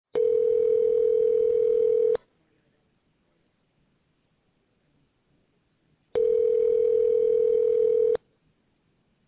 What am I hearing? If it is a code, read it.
Ringback tone (North America), 440 + 480 Hz.